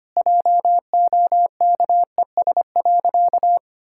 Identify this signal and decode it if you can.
Morse code, heard as a keyed tone, about 25 wpm, 700 Hz. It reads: JOKES.
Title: Pangram Challenge v2 - 25wpm